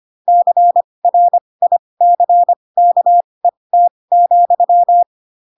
Morse code, heard as a keyed tone, about 25 wpm, spaced as for 20 wpm, 700 Hz. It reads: CRICKET,
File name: Pinocchio / Pinocchio ch4 15 25 ik1hge